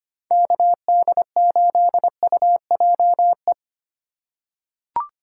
Morse code, heard as a keyed tone, about 25 wpm, 700 Hz. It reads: KD8UJE
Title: USA Call Signs - 6 Characters - 3x Character Spacing - 25wpm